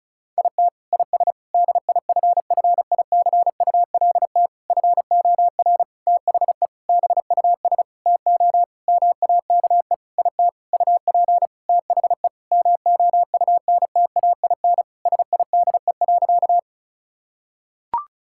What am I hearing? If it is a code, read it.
Morse code, heard as a keyed tone, about 34 wpm, 700 Hz. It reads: IT IS DIFFICULT FOR THE BUS TO MAKE IT UP THE MOUNTAIN SIDE.
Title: Sentences from Top 1000 Words - Review Part 6 - 35wpm